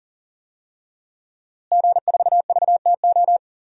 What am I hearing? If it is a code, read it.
Morse code, heard as a keyed tone, about 39 wpm, 700 Hz. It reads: G4VTO
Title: DXCC-100 Call Signs - 5 Characters - Repeated 3x - 40wpm